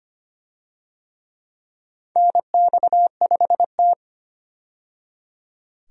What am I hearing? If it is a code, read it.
Morse code, heard as a keyed tone, about 25 wpm, 700 Hz. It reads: NX5T
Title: USA Call Signs - 4 Characters - 25wpm